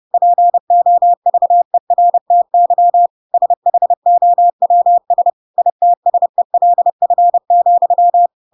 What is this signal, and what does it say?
Morse code, heard as a keyed tone, about 30 wpm, 700 Hz. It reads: POVERTY SHOWS ITSELF,
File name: Pinocchio / Pinocchio ch8 30 30 ik1hge